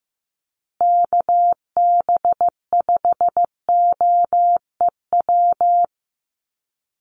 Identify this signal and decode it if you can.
Morse code, heard as a keyed tone, about 15 wpm, 700 Hz. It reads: KB5OEW
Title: USA Call Signs Repeated 3x 15wpm